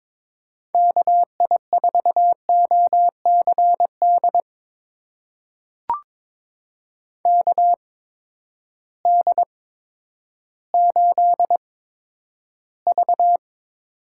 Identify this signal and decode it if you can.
Morse code, heard as a keyed tone, about 22 wpm, spaced as for 11 wpm, 700 Hz. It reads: KI4OCD KD8V
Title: USA Call Signs - 8x Character Spacing - 22wpm